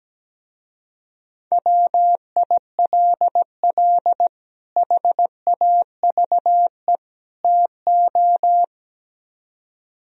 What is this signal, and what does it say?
Morse code, heard as a keyed tone, about 17 wpm, 700 Hz. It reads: WILL HAVE TO
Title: Speed-Racing - Sets of 3 Words 17wpm